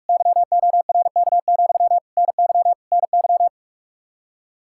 Morse code, heard as a keyed tone, about 41 wpm, 700 Hz. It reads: YORK, NY NY
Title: Practice QSO - Exchange 1 - Beginning - 45wpm